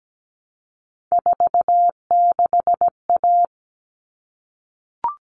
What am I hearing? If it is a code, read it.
Morse code, heard as a keyed tone, about 17 wpm, 700 Hz. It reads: 46A
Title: All Cut Numbers - 3-Digit 17wpm